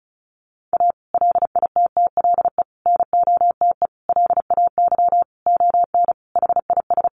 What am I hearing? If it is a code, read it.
Morse code, heard as a keyed tone, about 34 wpm, 700 Hz. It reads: A LITTLE NOTE LAY ON HIS
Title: Sentences from Top 1500 Words 35wpm